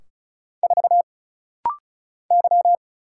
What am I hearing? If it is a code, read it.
Morse code, heard as a keyed tone, about 36 wpm, 700 Hz. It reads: 4 Y
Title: Single Letter-Number Alt - Rapid-Fire 35wpm